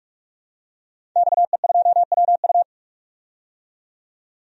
Morse code, heard as a keyed tone, about 47 wpm, 700 Hz. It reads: XE2WU
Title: DXCC-100 Call Signs - 5 Characters - 45wpm